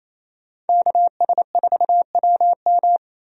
Morse code, heard as a keyed tone, about 28 wpm, 700 Hz. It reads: KS4WM
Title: USA Call Signs - 5 Characters - 28wpm